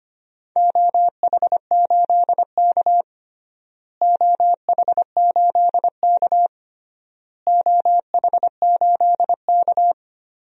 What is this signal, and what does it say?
Morse code, heard as a keyed tone, about 25 wpm, 700 Hz. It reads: OH8K OH8K OH8K
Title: DXCC-100 Call Signs - 4 Characters - Repeated 3x - 25wpm